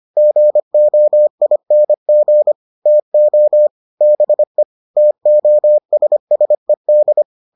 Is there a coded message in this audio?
Yes: Morse code from 0.2 to 7.2 s, heard as a keyed tone, about 25 wpm, 600 Hz. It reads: GOING TO BE TOSSED